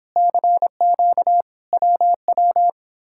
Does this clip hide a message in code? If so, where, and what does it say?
Yes: Morse code from 0.2 to 2.7 s, heard as a keyed tone, about 31 wpm, 700 Hz. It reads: CQ WW